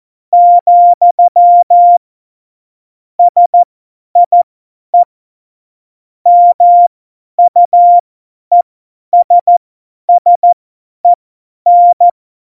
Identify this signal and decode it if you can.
Morse code, heard as a keyed tone, about 14 wpm, spaced as for 10 wpm, 700 Hz. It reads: , SIE MUESSEN
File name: Morsekurs / Witze / Witz 13 March 2018, 50 F70